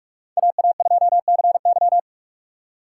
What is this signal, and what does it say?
Morse code, heard as a keyed tone, about 42 wpm, 700 Hz. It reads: AA2CY